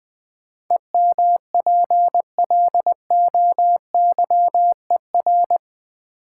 Morse code, heard as a keyed tone, about 20 wpm, 700 Hz. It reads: EMPLOYER
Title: Sentences from Top 1600 Words - New Word Drill - 20wpm